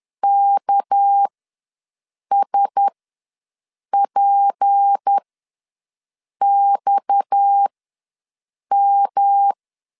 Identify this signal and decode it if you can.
Morse code, heard as a keyed tone, about 10 wpm, spaced as for 6 wpm, 800 Hz. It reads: KSPXM
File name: LEZIONI / 7 GF9Z / 7 G ws 10wpm 3843